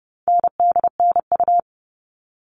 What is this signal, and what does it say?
Morse code, heard as a keyed tone, about 30 wpm, 700 Hz. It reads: NDNU
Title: All Cut Numbers - 4-Digit 30wpm